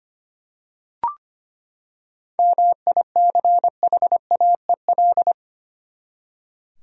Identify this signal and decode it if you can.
Morse code, heard as a keyed tone, about 25 wpm, 700 Hz. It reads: MICHAEL
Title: USA Common Names - 25wpm